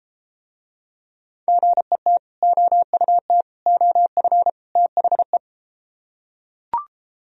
Morse code, heard as a keyed tone, about 33 wpm, 700 Hz. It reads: GET OUT OF THE